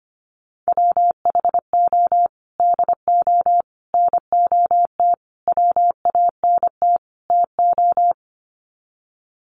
Morse code, heard as a keyed tone, about 25 wpm, 700 Hz. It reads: WHO DO NOT WANT TO
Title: Sets of 5 Words Encore2 25wpm